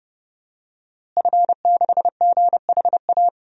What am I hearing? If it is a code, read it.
Morse code, heard as a keyed tone, about 30 wpm, 700 Hz. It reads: F6GHA
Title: DXCC-100 Call Signs - 5 Characters - Repeated 2x - 30wpm